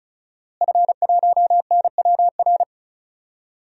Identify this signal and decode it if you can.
Morse code, heard as a keyed tone, about 36 wpm, 700 Hz. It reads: F1NWR
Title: DXCC-100 Call Signs - 5 Characters - Repeated 2x - 35wpm